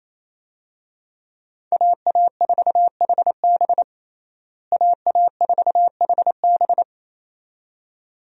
Morse code, heard as a keyed tone, about 28 wpm, 700 Hz. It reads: AA4HB AA4HB
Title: USA Call Signs - 5 Characters - Repeated 2x - 28wpm